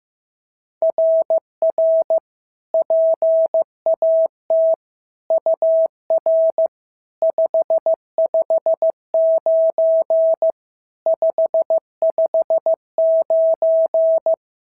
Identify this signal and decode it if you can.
Morse code, heard as a keyed tone, about 15 wpm, 650 Hz. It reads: RR PAT UR 559 559
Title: SOTA QSO Practice 15wpm